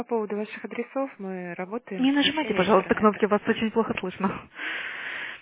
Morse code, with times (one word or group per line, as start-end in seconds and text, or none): none